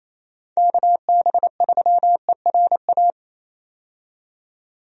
Morse code, heard as a keyed tone, about 28 wpm, 700 Hz. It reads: KB3ERA
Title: USA Call Signs - 6 Characters - 28wpm